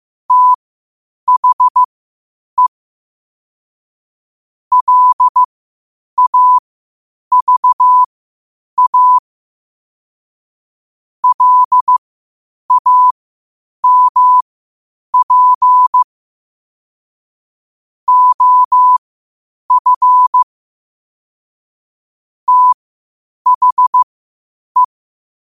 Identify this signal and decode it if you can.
Morse code, heard as a keyed tone, about 15 wpm, spaced as for 8 wpm, 1000 Hz. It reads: THE LAVA LAMP OF THE
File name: qotdfiles / 8wpm